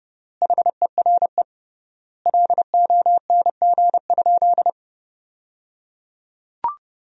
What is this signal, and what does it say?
Morse code, heard as a keyed tone, about 30 wpm, 700 Hz. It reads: HERE LONG?
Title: 3x Word-Spacing - Sentences from Top 200 Words - 30wpm